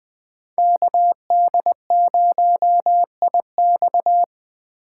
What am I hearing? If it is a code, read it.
Morse code, heard as a keyed tone, about 21 wpm, 700 Hz. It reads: KD0IX